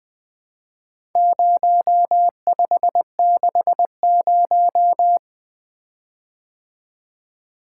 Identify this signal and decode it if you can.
Morse code, heard as a keyed tone, about 20 wpm, 700 Hz. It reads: 0560
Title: Four Letter-Number Combinations - 20wpm